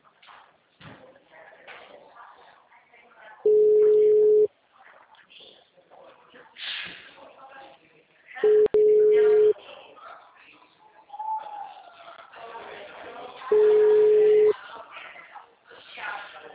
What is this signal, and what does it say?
Ringback tone (Europe, 425 Hz), with speech.